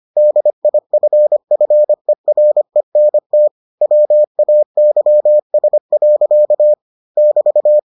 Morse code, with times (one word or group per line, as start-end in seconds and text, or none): 0.2-3.5 DIFFERENT
3.8-6.7 WAYS.
7.2-7.8 <BT>